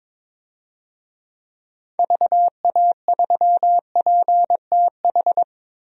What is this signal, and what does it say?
Morse code, heard as a keyed tone, about 22 wpm, 700 Hz. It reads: VA3PTH
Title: DXCC-100 Call Signs - 6 Characters - Repeated 2x - 22wpm